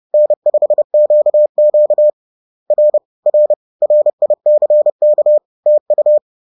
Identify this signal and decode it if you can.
Morse code, heard as a keyed tone, about 30 wpm, 600 Hz. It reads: N5QQ R R RICK TU
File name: 30wpm-4